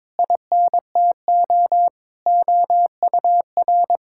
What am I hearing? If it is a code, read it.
Morse code, heard as a keyed tone, about 22 wpm, 700 Hz. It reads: INTO OUR
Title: Sentences from Top 1000 Words - Review Part 6 - 22wpm